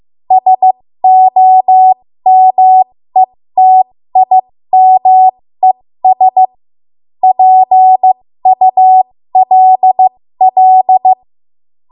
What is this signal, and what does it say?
Morse code, heard as a keyed tone, about 15 wpm, spaced as for 13 wpm, 750 Hz. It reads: SOMETIMES PULL
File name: Morse / 13WPM